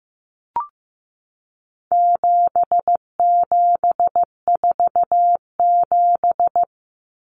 Morse code, heard as a keyed tone, about 15 wpm, 700 Hz. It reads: 7747
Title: Common Cut Numbers - 4-Digit 15wpm